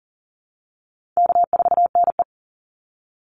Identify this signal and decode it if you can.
Morse code, heard as a keyed tone, about 40 wpm, 700 Hz. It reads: K4NE
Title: Easy Call Signs Repeated 2x - 40wpm